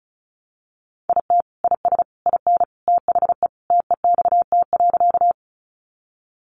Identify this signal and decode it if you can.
Morse code, heard as a keyed tone, about 34 wpm, 700 Hz. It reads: IT IS IN THE TEXT.
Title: Sentences from Top 1200 Words 35wpm